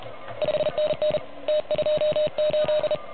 Morse code, heard as a keyed tone, about 32 wpm, 630 Hz. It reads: 5NN T28
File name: truSDX / MWC29.8